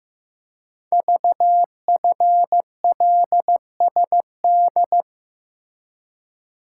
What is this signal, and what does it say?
Morse code, heard as a keyed tone, about 15 wpm, 700 Hz. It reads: VFLSD
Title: Code Groups - 5 Letters - 15wpm